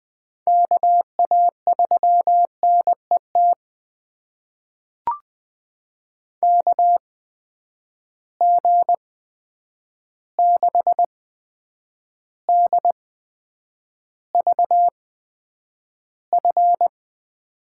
Morse code, heard as a keyed tone, about 20 wpm, spaced as for 9 wpm, 700 Hz. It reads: KA3NET KG6DVF
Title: USA Call Signs - 6 Characters - 8x Character Spacing - 20wpm